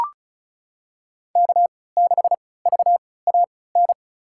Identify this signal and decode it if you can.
Morse code, heard as a keyed tone, about 36 wpm, spaced as for 20 wpm, 700 Hz. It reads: K6VAN
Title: USA Call Signs - 3x Character Spacing - 35wpm